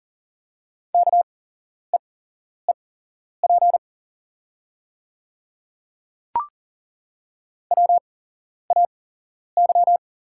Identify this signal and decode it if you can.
Morse code, heard as a keyed tone, about 40 wpm, spaced as for 11 wpm, 700 Hz. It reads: KEEP WAY
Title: 8x Character Spacing - Top 300 Words - 40wpm